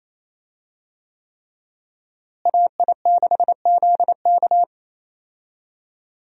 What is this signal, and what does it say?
Morse code, heard as a keyed tone, about 28 wpm, 700 Hz. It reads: AI6ZK
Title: USA Call Signs - 5 Characters - Repeated 3x - 28wpm